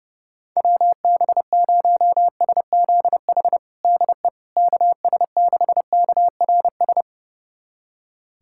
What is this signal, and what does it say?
Morse code, heard as a keyed tone, about 30 wpm, 700 Hz. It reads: WB0SZH DE KS6KRS
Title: USA Callsign DE Callsign 30wpm